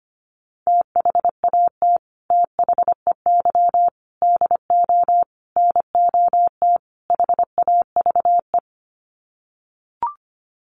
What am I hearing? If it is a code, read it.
Morse code, heard as a keyed tone, about 25 wpm, 700 Hz. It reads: THAT THEY DO NOT HAVE
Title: Sets of 5 Words Encore2 25wpm